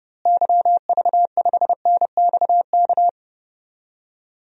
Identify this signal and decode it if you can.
Morse code, heard as a keyed tone, about 30 wpm, 700 Hz. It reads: YV5NXK
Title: DXCC-100 Call Signs - 6 Characters - Repeated 2x - 30wpm